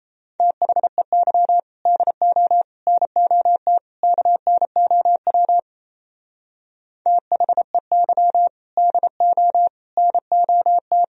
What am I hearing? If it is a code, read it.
Morse code, heard as a keyed tone, about 36 wpm, 700 Hz. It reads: THEY DO NOT KNOW THEY DO NOT